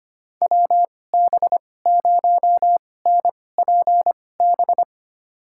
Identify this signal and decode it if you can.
Morse code, heard as a keyed tone, about 25 wpm, spaced as for 18 wpm, 700 Hz. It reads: WB0NPB
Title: USA Call Signs - 2x Character Spacing - 25wpm